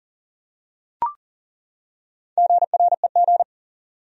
Morse code, heard as a keyed tone, about 38 wpm, 700 Hz. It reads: GREG
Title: USA Common Names - Max 5 Letters - 40wpm